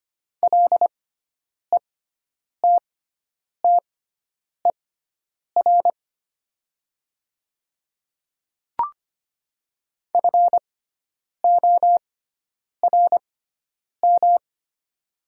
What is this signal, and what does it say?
Morse code, heard as a keyed tone, about 25 wpm, spaced as for 9 wpm, 700 Hz. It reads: LETTER FORM
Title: 6x Character Spacing - Top 300 Words - 25wpm